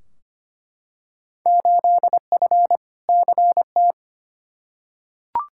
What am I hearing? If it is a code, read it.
Morse code, heard as a keyed tone, about 25 wpm, 700 Hz. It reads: 8F CT
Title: Field Day Exchanges Encore - 25wpm